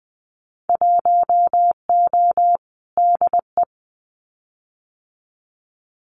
Morse code, heard as a keyed tone, about 20 wpm, 700 Hz. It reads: 1O DE
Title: Winter Field Day Exchanges 20wpm